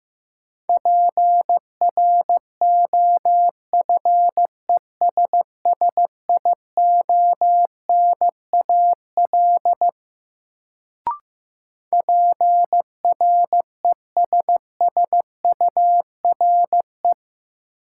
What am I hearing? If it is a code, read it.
Morse code, heard as a keyed tone, about 15 wpm, 700 Hz. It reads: PROFESSIONAL PRESSURE
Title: Sentences from Top 1400 Words - New Word Drill Alt - 15wpm